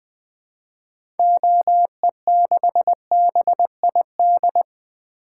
Morse code, heard as a keyed tone, about 20 wpm, 700 Hz. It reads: OE6BID
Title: 2023 SOTA Activator Call Signs - 20wpm